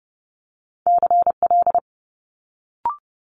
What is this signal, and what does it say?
Morse code, heard as a keyed tone, about 31 wpm, 700 Hz. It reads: CL
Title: Multi-letter Phonemes Quick Fire 30wpm